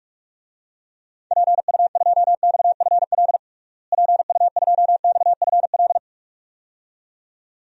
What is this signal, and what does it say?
Morse code, heard as a keyed tone, about 46 wpm, 700 Hz. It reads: PU2XFL PU2XFL
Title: DXCC-100 Call Signs - 6 Characters - Repeated 2x - 45wpm